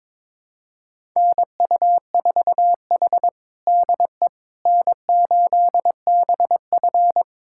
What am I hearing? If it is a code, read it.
Morse code, heard as a keyed tone, about 21 wpm, 700 Hz. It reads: NU4H DE N8BF
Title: USA Callsign DE Callsign - 4 Characters - 22wpm